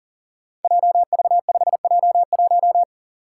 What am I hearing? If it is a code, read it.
Morse code, heard as a keyed tone, about 42 wpm, 700 Hz. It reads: JV5J1
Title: Five Challenging Letter-Number Combinations 40wpm